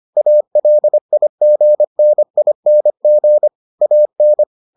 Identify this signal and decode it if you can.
Morse code, heard as a keyed tone, about 25 wpm, 600 Hz. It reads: ALIGNING AN